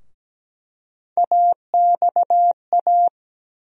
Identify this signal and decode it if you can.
Morse code, heard as a keyed tone, about 17 wpm, 700 Hz. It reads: AXA
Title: USA Call Sign Postfixes - 3 Letters 17wpm